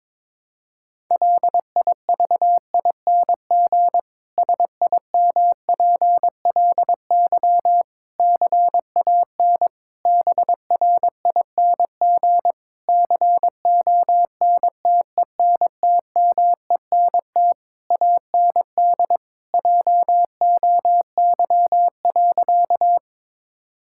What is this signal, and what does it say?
Morse code, heard as a keyed tone, about 22 wpm, 700 Hz. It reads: LIVING SIMPLY CAN BRING CONTENTMENT AND JOY.
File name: Difficult Sentences with Hints - 22wpm